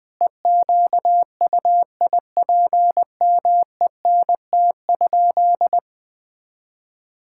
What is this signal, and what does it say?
Morse code, heard as a keyed tone, about 21 wpm, 700 Hz. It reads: EQUIPMENT?
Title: Sentences from Top 1000 Words - Review Part 5 - 20wpm